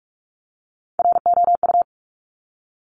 Morse code, heard as a keyed tone, about 46 wpm, 700 Hz. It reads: ROU
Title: Top Three Letter Combinations in Words 45wpm